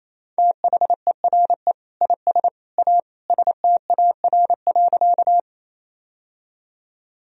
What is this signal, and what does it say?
Morse code, heard as a keyed tone, about 28 wpm, 700 Hz. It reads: THERE IS A STAR.
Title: Sentences from Top 1000 Words - Review Part 2 - 28wpm